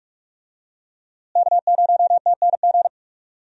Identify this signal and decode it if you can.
Morse code, heard as a keyed tone, about 42 wpm, 700 Hz. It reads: K0TNG
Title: USA Call Signs - 5 Characters - Repeated 2x - 45wpm